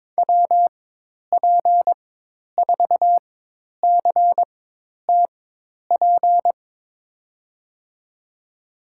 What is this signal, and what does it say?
Morse code, heard as a keyed tone, about 22 wpm, spaced as for 10 wpm, 700 Hz. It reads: WP4CTP
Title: USA Call Signs - 6 Characters - 4x Character Spacing - 22wpm